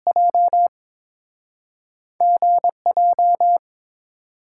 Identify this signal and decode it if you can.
Morse code, heard as a keyed tone, about 31 wpm, spaced as for 27 wpm, 700 Hz. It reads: J GJ
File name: Speed-Racing - Two Letter-Number Combinations - 22wpm